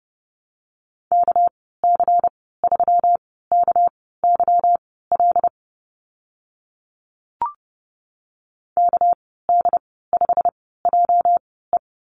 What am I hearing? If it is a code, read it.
Morse code, heard as a keyed tone, about 30 wpm, spaced as for 17 wpm, 700 Hz. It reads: KC3KYL KD5JE
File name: Lesson 37g - Letter x - Call Signs 30wpm